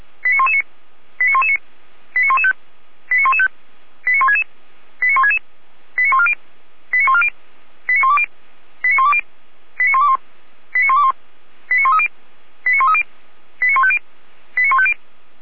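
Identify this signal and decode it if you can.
Five-tone selective call (ZVEI), heard as quick runs of five tones, tones of about 70 ms. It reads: 89119 89119 89115 89115 89166 89166 89140 89140 89120 89120 89121 89121 89130 89130 89150 89150
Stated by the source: ZVEI five-tone call 89092 received on SDR.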